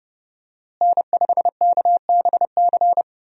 Morse code, heard as a keyed tone, about 30 wpm, 700 Hz. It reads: N5KBC
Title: USA Call Signs - 5 Characters - Repeated 3x - 30wpm